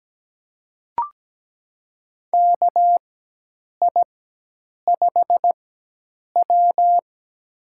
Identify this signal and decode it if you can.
Morse code, heard as a keyed tone, about 17 wpm, spaced as for 8 wpm, 700 Hz. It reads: KI5W
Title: USA Call Signs - 4 Characters - 4x Character Spacing - 17wpm